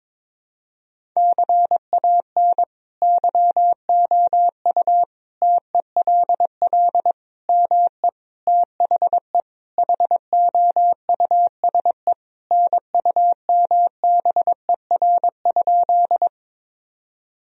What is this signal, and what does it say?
Morse code, heard as a keyed tone, about 22 wpm, 700 Hz. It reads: CAN YOU TELL ME THE HOUSE NUMBER?
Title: Sentences from Top 1000 Words - Review Part 1 - 22wpm